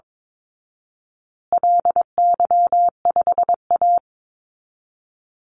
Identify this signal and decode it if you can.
Morse code, heard as a keyed tone, about 22 wpm, 700 Hz. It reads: LY5A
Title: Call Signs Repeated 2x - 22wpm